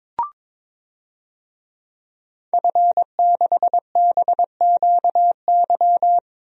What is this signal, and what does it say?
Morse code, heard as a keyed tone, about 21 wpm, 700 Hz. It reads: F6BQY
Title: DXCC-100 Call Signs - 5 Characters - Repeated 3x - 22wpm